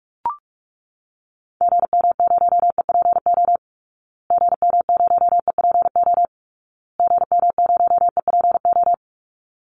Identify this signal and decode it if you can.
Morse code, heard as a keyed tone, about 44 wpm, 700 Hz. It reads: GM0EPO GM0EPO GM0EPO
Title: Call Signs Repeated 2x - 45wpm